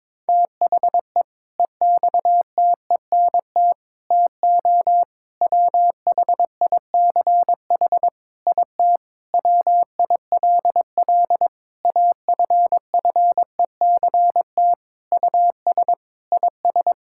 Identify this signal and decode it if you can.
Morse code, heard as a keyed tone, about 21 wpm, 700 Hz. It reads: THE EXTENT TO WHICH IT WILL AFFECT US IS
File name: practice / 22wpm / Sentences with 4-Word Phrases - 22wpm